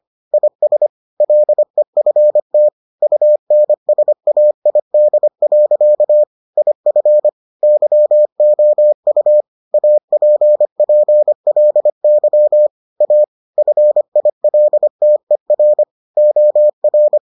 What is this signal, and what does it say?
Morse code, heard as a keyed tone, about 25 wpm, 600 Hz. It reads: IS LEFT UNSAID. IF YOU APPLY A FILTER OR